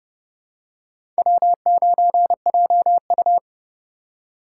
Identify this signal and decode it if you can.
Morse code, heard as a keyed tone, about 29 wpm, 700 Hz. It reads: W9JU
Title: USA Call Signs - 4 Characters - 2x Character Spacing - 30wpm